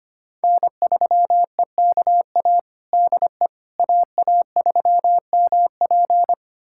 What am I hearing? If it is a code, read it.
Morse code, heard as a keyed tone, about 25 wpm, 700 Hz. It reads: N3EKA DE AA3MP
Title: USA Callsign DE Callsign - 5 Characters - 25wpm